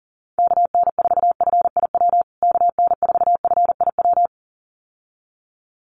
Morse code, heard as a keyed tone, about 40 wpm, 700 Hz. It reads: KN4FIW KN4FIW
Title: USA Call Signs Repeated 2x 40wpm